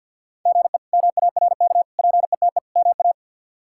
Morse code, heard as a keyed tone, about 47 wpm, 700 Hz. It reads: GE MARK PETE MA
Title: K1USN SST QSO Practice - 50wpm